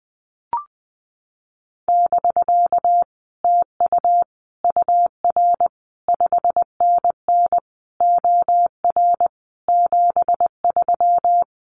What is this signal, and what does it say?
Morse code, heard as a keyed tone, about 20 wpm, 700 Hz. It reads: <BK> TU UR 5NN OR 73
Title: Common QSO Phrases Encore 20wpm